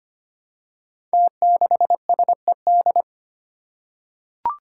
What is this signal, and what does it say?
Morse code, heard as a keyed tone, about 25 wpm, 700 Hz. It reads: T6SED
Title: Five Challenging Letter-Number Combinations 25wpm